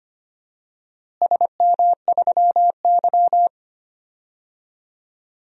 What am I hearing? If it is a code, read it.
Morse code, heard as a keyed tone, about 25 wpm, 700 Hz. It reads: SM3Y
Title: DXCC-100 Call Signs - 4 Characters - Repeated 2x - 25wpm